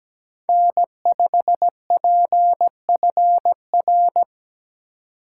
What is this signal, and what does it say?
Morse code, heard as a keyed tone, about 17 wpm, 700 Hz. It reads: N5PFR
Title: USA Call Signs - 5 Characters - 4x Character Spacing - 17wpm